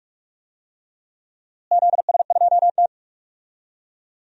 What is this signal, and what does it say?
Morse code, heard as a keyed tone, about 43 wpm, 700 Hz. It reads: ZS2T
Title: DXCC-100 Call Signs - 4 Characters - Repeated 2x - 45wpm